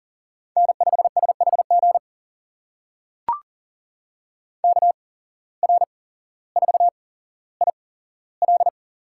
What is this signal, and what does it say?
Morse code, heard as a keyed tone, about 40 wpm, spaced as for 17 wpm, 700 Hz. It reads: N5SHG KR4IL
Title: USA Call Signs - 5 Characters - 8x Character Spacing - 40wpm